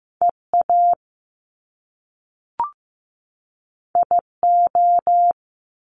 Morse code, heard as a keyed tone, about 15 wpm, 700 Hz. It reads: EA IO
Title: Top Two Letter Combinations in Words 15wpm